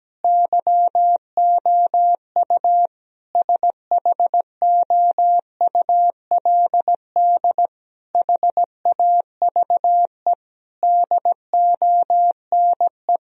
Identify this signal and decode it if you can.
Morse code, heard as a keyed tone, about 17 wpm, 700 Hz. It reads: YOU SHOULD HAVE DONE
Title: Sentences from Top 1000 Words - Review Part 3 - 17wpm